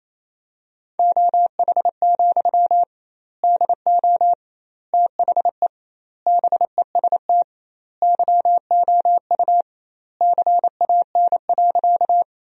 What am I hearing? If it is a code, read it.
Morse code, heard as a keyed tone, about 28 wpm, 700 Hz. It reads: OH, DO THE BEST YOU CAN.
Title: 2x Word-Spacing - Sentences from Top 500 Words - 28wpm